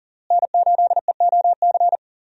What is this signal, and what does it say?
Morse code, heard as a keyed tone, about 40 wpm, 700 Hz. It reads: N8EOC
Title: USA Call Signs - 5 Characters - Repeated 2x - 40wpm